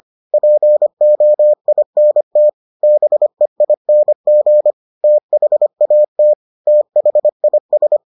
Morse code, heard as a keyed tone, about 25 wpm, 600 Hz. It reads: POINT BEING THAT THIS